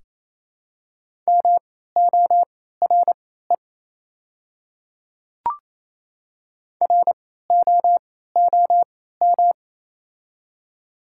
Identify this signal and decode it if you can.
Morse code, heard as a keyed tone, about 28 wpm, spaced as for 16 wpm, 700 Hz. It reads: MORE ROOM
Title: 3x Character Spacing - Top 300 Words - 28wpm